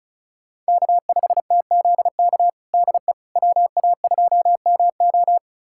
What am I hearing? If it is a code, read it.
Morse code, heard as a keyed tone, about 36 wpm, 700 Hz. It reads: K5TZK DE WA2MO
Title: USA Callsign DE Callsign - 5 Characters - 35wpm